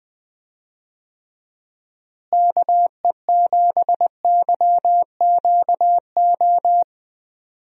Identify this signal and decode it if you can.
Morse code, heard as a keyed tone, about 21 wpm, 700 Hz. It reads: KE7YQO
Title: USA Call Signs - 6 Characters - Repeated 2x - 20wpm